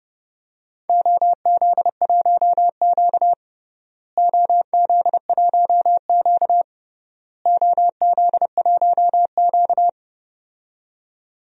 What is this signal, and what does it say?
Morse code, heard as a keyed tone, about 30 wpm, 700 Hz. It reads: OZ1Q OZ1Q OZ1Q
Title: DXCC-100 Call Signs - 4 Characters - Repeated 3x - 30wpm